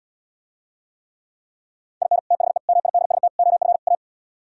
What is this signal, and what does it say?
Morse code, heard as a keyed tone, about 77 wpm, 700 Hz. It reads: IN THE MIDST OF A